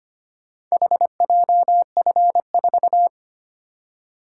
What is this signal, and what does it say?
Morse code, heard as a keyed tone, about 25 wpm, 700 Hz. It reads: HJF4
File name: practice / 25wpm / Four Letter-Number Combinations - 25wpm